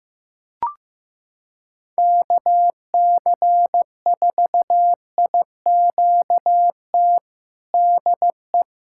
Morse code, heard as a keyed tone, about 15 wpm, 700 Hz. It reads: KC4IQT DE
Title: USA Callsign DE Callsign 15wpm